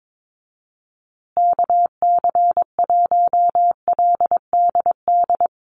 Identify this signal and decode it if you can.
Morse code, heard as a keyed tone, about 22 wpm, 700 Hz. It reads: KC1LDD
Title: USA Call Signs Repeated 2x 22wpm